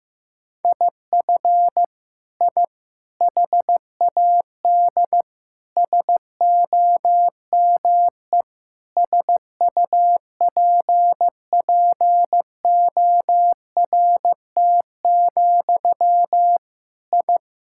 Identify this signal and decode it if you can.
Morse code, heard as a keyed tone, about 15 wpm, 700 Hz. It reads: IF I HAD SOME SUPPORT, I
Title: Sentences from Top 1000 Words - Review Part 5 - 15wpm